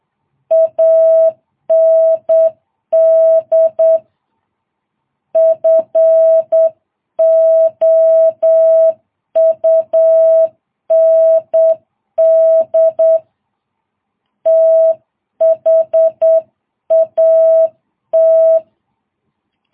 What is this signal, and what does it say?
Morse code, heard as a keyed tone, about 8 wpm, 650 Hz. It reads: AND FOUND THAT
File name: jh8cbh